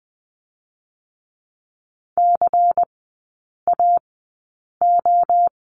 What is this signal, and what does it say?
Morse code, heard as a keyed tone, about 20 wpm, spaced as for 8 wpm, 700 Hz. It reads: CAO
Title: 145 Lesson 24e - Letter y - Three Characters 20wpm